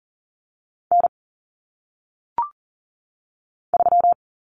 Morse code, heard as a keyed tone, about 40 wpm, 700 Hz. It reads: N 3
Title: Common Cut Numbers - 1-Digit 40wpm